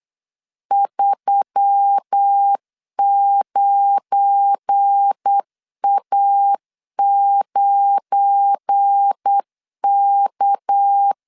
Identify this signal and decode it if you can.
Morse code, heard as a keyed tone, about 8 wpm, 800 Hz. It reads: 39A9K
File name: LEZIONI / 7 GF9Z / 7 GF9 8wpm 3545